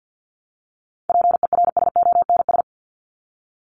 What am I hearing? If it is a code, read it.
Morse code, heard as a keyed tone, about 50 wpm, 700 Hz. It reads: PERSONS
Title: ES:S Suffix - Top 100 Words with Base Words 50wpm